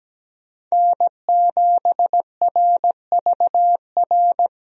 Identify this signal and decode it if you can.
Morse code, heard as a keyed tone, about 17 wpm, 700 Hz. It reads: N7RVR
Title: USA Call Signs - 5 Characters - Repeated 2x - 17wpm